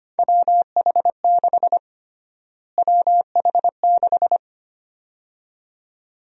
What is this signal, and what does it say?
Morse code, heard as a keyed tone, about 25 wpm, 700 Hz. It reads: WH6 WH6
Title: USA Call Sign Prefixes - Repeated 2x - 25wpm